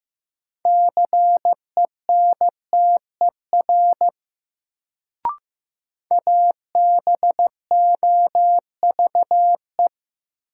Morse code, heard as a keyed tone, about 16 wpm, 700 Hz. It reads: CENTER ABOVE
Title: Sentences from Top 1000 Words - Review Part 2 - New Word Drill - 15wpm